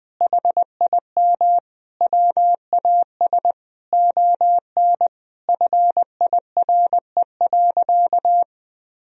Morse code, heard as a keyed tone, about 20 wpm, 700 Hz. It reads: HIM WAS ON FIRE.
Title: Sentences from Top 1000 Words - Review Part 3 - 20wpm